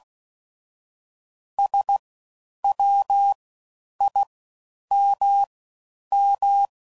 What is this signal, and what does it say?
Morse code, heard as a keyed tone, about 16 wpm, spaced as for 9 wpm, 800 Hz. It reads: SWIMM
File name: Suffixes / Special INGs / Fast Ings